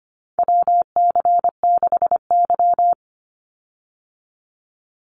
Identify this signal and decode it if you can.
Morse code, heard as a keyed tone, about 25 wpm, 700 Hz. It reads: WC6Y
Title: Call Signs Repeated 2x - 25wpm